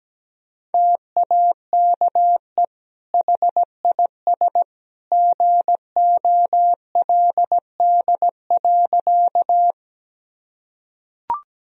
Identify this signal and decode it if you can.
Morse code, heard as a keyed tone, about 17 wpm, 700 Hz. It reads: TAKE HIS GOLD.
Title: Sentences from Top 1000 Words - Review Part 3 - 17wpm